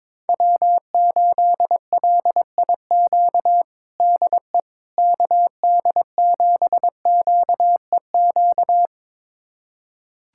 Morse code, heard as a keyed tone, about 22 wpm, 700 Hz. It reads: W8LIQ DE KD7QEQ
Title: USA Callsign DE Callsign 22wpm